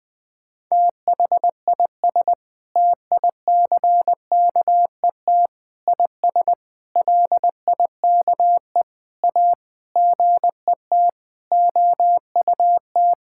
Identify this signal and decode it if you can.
Morse code, heard as a keyed tone, about 20 wpm, 700 Hz. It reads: THIS TICKET IS LIKE A GET OUT
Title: Sentences with 7-Word Phrases - 20wpm